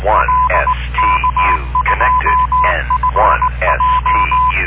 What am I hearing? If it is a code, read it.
Morse code, heard as a keyed tone, about 19 wpm, 1010 Hz, with noise. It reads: L K2FJ